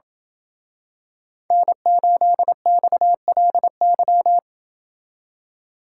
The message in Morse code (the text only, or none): N8XLY